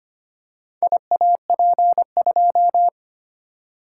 Morse code, heard as a keyed tone, about 25 wpm, 700 Hz. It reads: IAP2
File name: Four Letter-Number Combinations - 25wpm